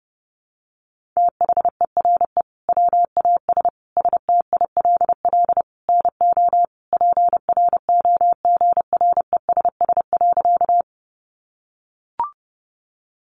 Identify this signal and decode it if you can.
Morse code, heard as a keyed tone, about 30 wpm, 700 Hz. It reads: THERE WAS STILL NO PROGRESS.
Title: Sentences from Top 1400 Words 30wpm